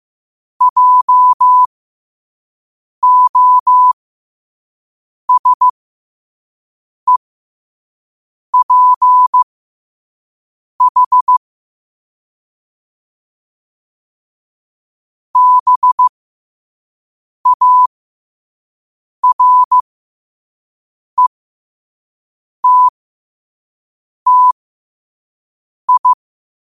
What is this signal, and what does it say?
Morse code, heard as a keyed tone, about 15 wpm, spaced as for 5 wpm, 1000 Hz. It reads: JOSEPH BARETTI
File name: qotdfiles / 5wpm